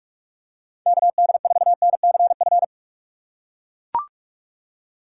Morse code, heard as a keyed tone, about 43 wpm, 700 Hz. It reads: KD4NCF